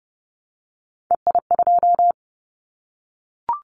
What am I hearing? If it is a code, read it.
Morse code, heard as a keyed tone, about 30 wpm, 700 Hz. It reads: EI2
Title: Callsign Prefixes for Top 50 DXCC - 30wpm